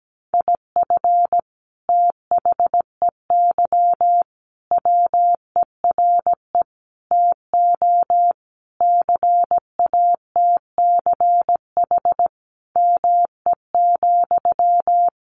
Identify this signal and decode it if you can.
Morse code, heard as a keyed tone, about 17 wpm, 700 Hz. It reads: IF THEY WERE TO CATCH ME,
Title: Sentences from Top 1500 Words 17wpm